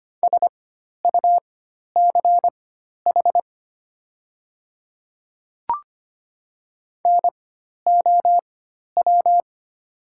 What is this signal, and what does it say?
Morse code, heard as a keyed tone, about 25 wpm, spaced as for 12 wpm, 700 Hz. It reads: SUCH NOW
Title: 4x Character Spacing - Top 200 Words - 25wpm